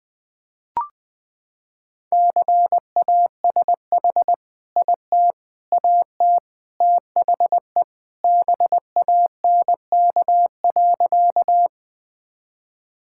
Morse code, heard as a keyed tone, about 21 wpm, 700 Hz. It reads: CASH IT AT THE BANK.
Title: Sentences from Top 1000 Words - Review Part 5 - 20wpm